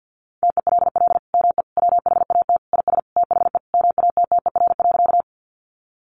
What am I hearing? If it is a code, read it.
Morse code, heard as a keyed tone, about 49 wpm, 700 Hz. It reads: TELL ME WHAT IS THE MATTER.
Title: Sentences from Top 700 Words 50wpm